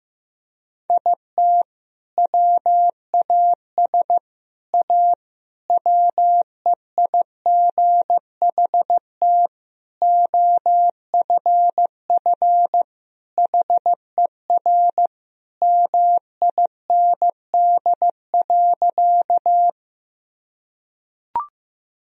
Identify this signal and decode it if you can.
Morse code, heard as a keyed tone, about 15 wpm, 700 Hz. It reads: IT WAS A WEIGHT OFF HER MIND.
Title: Sentences from Top 1000 Words - Review Part 3 - 15wpm